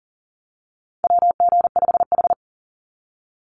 Morse code, heard as a keyed tone, about 40 wpm, 700 Hz. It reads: WG5H
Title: Call Signs Repeated 3x - 40wpm